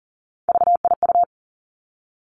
Morse code, heard as a keyed tone, about 38 wpm, 700 Hz. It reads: VIU
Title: Three Challenging Letter-Number Combinations 40wpm